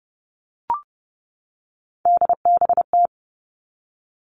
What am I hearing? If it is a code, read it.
Morse code, heard as a keyed tone, about 30 wpm, 700 Hz. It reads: DBT
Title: All Cut Numbers - 3-Digit 30wpm